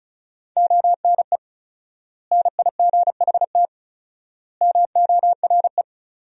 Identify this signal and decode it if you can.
Morse code, heard as a keyed tone, about 34 wpm, 700 Hz. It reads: ONE NIGHT MORE
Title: 4x Word-Spacing - Sentences from Top 300 Words - 35wpm